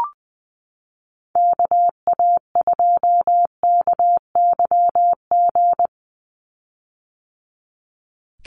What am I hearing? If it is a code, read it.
Morse code, heard as a keyed tone, about 20 wpm, 700 Hz. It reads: KA2KYG
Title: USA Call Signs 20wpm